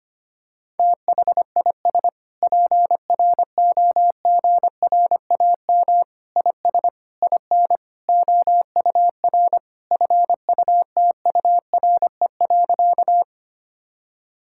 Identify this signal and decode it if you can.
Morse code, heard as a keyed tone, about 25 wpm, 700 Hz. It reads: THIS PROGRAM IS IN OUR FUTURE.
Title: Sentences from Top 1000 Words - Review Part 5 - 25wpm